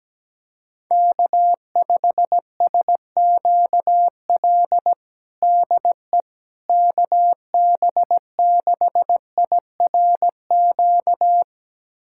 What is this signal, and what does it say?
Morse code, heard as a keyed tone, about 17 wpm, 700 Hz. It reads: K5SQL DE KB6IRQ
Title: USA Callsign DE Callsign 17wpm